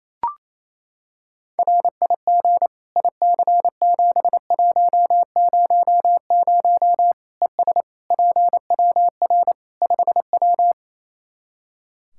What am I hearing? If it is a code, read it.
Morse code, heard as a keyed tone, about 28 wpm, 700 Hz. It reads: RIG IC7100 ES PWR 5W
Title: Practice QSO - Exchange 2 - Info - 28wpm